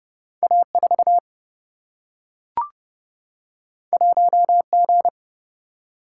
Morse code, heard as a keyed tone, about 31 wpm, 700 Hz. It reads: A 4 1 G
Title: Single-Letter Single-Number 30wpm